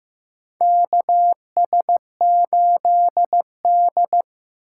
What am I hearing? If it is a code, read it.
Morse code, heard as a keyed tone, about 15 wpm, 700 Hz. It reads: KS8D
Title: USA Call Signs - 4 Characters - 3x Character Spacing - 15wpm